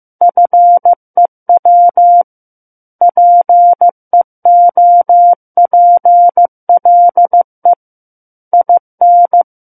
Morse code, heard as a keyed tone, about 15 wpm, 700 Hz. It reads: FEW PEOPLE IN
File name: Book One / 06-Chapter 6 - 15 WPM